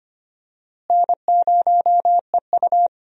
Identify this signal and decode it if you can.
Morse code, heard as a keyed tone, about 25 wpm, 700 Hz. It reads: N0EU
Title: USA Call Signs - 4 Characters - Repeated 3x - 25wpm